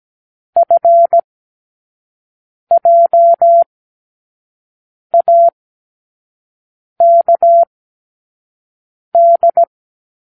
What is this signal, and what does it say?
Morse code, heard as a keyed tone, about 17 wpm, spaced as for 5 wpm, 680 Hz. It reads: FJAKD
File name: k7qo / Lesson18(A-L)